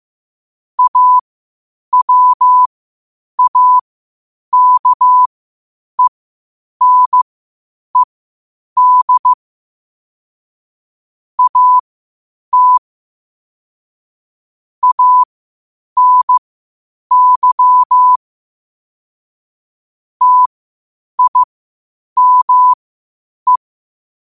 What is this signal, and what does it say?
Morse code, heard as a keyed tone, about 15 wpm, spaced as for 8 wpm, 1000 Hz. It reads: AWAKENED AT ANY TIME
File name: qotdfiles / 8wpm